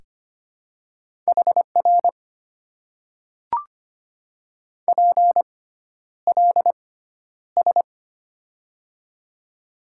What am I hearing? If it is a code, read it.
Morse code, heard as a keyed tone, about 25 wpm, spaced as for 11 wpm, 700 Hz. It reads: HR PLS
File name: Top 90 QSO Elements - 6x Character Spacing - 25wpm